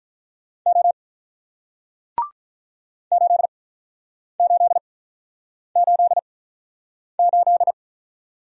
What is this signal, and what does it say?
Morse code, heard as a keyed tone, about 46 wpm, spaced as for 9 wpm, 700 Hz. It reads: O 8888